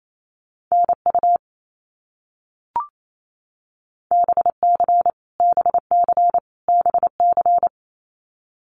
Morse code, heard as a keyed tone, about 28 wpm, 700 Hz. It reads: NU BC BC BC
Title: Canadian Provinces and Territories - Repeated 3x - 28wpm